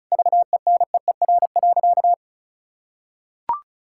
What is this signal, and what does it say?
Morse code, heard as a keyed tone, about 36 wpm, 700 Hz. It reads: VENEER.
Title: Pangram Challenge v2 - 35wpm